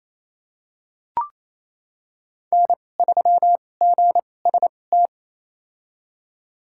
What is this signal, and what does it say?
Morse code, heard as a keyed tone, about 28 wpm, spaced as for 20 wpm, 700 Hz. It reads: N3GST
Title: USA Call Signs - 5 Characters - 2x Character Spacing - 28wpm